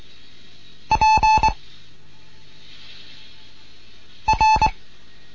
Morse code, heard as a keyed tone, about 23 wpm, 870 Hz. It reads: P R